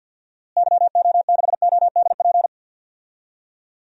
Morse code, heard as a keyed tone, about 48 wpm, 700 Hz. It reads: YO6ODP
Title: DXCC-100 Call Signs - 6 Characters - Repeated 2x - 50wpm